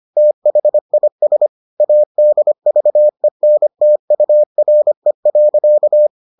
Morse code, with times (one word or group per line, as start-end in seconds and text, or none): 0.2-1.5 THIS
1.8-6.1 ADVENTURE.